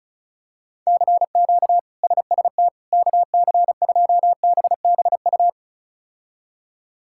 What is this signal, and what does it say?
Morse code, heard as a keyed tone, about 34 wpm, 700 Hz. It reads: CQ SST KC2BBU
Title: K1USN SST QSO Practice - 35wpm